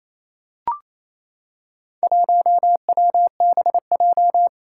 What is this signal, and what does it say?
Morse code, heard as a keyed tone, about 27 wpm, 700 Hz. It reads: 1WBJ
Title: Four Challenging Letter-Number Combinations 28wpm